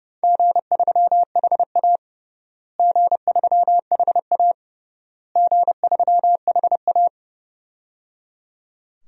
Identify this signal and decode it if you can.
Morse code, heard as a keyed tone, about 30 wpm, 700 Hz. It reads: G3HA G3HA G3HA
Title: DXCC-100 Call Signs - 4 Characters - Repeated 3x - 30wpm